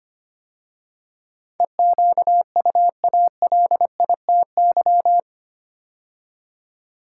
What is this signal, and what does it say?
Morse code, heard as a keyed tone, about 25 wpm, 700 Hz. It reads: EQUALITY
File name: 8 Letter Words - First 3 Letters Given - 25wpm